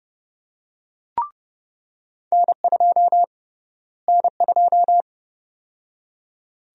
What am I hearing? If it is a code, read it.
Morse code, heard as a keyed tone, about 30 wpm, 700 Hz. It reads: N2 N2
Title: USA Call Sign Prefixes - Repeated 2x - 30wpm